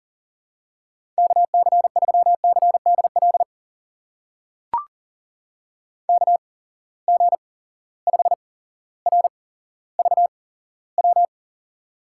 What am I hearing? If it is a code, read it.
Morse code, heard as a keyed tone, about 40 wpm, spaced as for 17 wpm, 700 Hz. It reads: KC3CDL KG5RVW
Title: USA Call Signs - 6 Characters - 8x Character Spacing - 40wpm